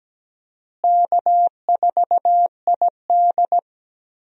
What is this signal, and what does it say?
Morse code, heard as a keyed tone, about 17 wpm, 700 Hz. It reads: K4ID
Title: USA Call Signs - 4 Characters - Repeated 2x - 17wpm